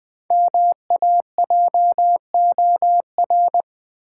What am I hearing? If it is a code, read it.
Morse code, heard as a keyed tone, about 21 wpm, 700 Hz. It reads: MAJOR
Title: Five Letter Words - 20wpm